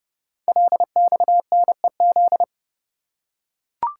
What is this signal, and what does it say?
Morse code, heard as a keyed tone, about 29 wpm, 700 Hz. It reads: LXNEZ
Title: Code Groups - 5 Letters - 30wpm